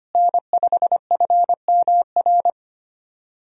Morse code, heard as a keyed tone, about 25 wpm, 700 Hz. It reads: N5FMR
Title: USA Call Signs - 5 Characters - Repeated 3x - 25wpm